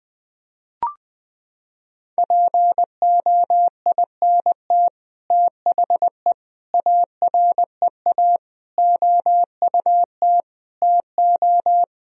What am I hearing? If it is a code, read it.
Morse code, heard as a keyed tone, about 21 wpm, 700 Hz. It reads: POINT THE AREA OUT TO
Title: Sentences from Top 1000 Words - Review Part 2 - 20wpm